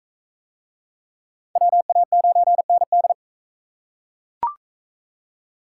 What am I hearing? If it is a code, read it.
Morse code, heard as a keyed tone, about 40 wpm, 700 Hz. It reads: WA9ND